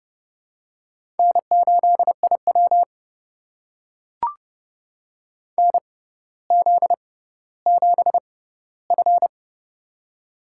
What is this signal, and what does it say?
Morse code, heard as a keyed tone, about 30 wpm, spaced as for 15 wpm, 700 Hz. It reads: N8IW NZ7F
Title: USA Call Signs - 4 Characters - 6x Character Spacing - 30wpm